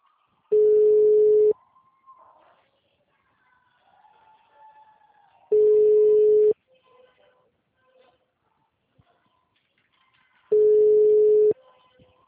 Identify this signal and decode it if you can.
Ringback tone (Europe, 425 Hz).
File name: out- -2007- - - .33